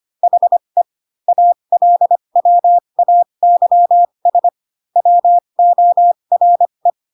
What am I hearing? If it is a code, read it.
Morse code, heard as a keyed tone, about 25 wpm, spaced as for 22 wpm, 700 Hz. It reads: HE ALWAYS WORE
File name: Pinocchio / Pinocchio ch2 18 25 ik1hge